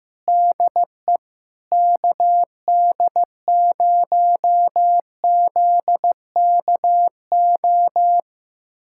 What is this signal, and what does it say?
Morse code, heard as a keyed tone, about 15 wpm, 700 Hz. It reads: DE KD0ZKO
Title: USA Callsign DE Callsign - 6 Characters - 15wpm